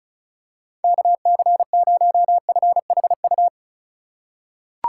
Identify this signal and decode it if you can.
Morse code, heard as a keyed tone, about 34 wpm, 700 Hz. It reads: KC0FHU
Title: USA Call Signs - 6 Characters - 3x Character Spacing - 35wpm